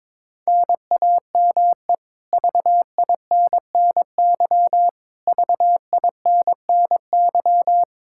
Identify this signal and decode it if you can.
Morse code, heard as a keyed tone, about 22 wpm, 700 Hz. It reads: NAME VINNY VINNY
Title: Practice QSO - Exchange 1 - Beginning - 22wpm